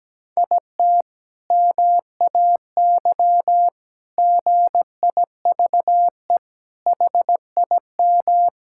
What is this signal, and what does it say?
Morse code, heard as a keyed tone, about 17 wpm, 700 Hz. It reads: IT MAY GIVE HIM
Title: Sentences from Top 1600 Words 17wpm